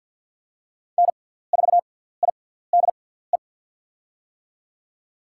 Morse code, heard as a keyed tone, about 47 wpm, spaced as for 17 wpm, 700 Hz. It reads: N4IDE
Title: USA Call Signs - 5 Characters - 6x Character Spacing - 50wpm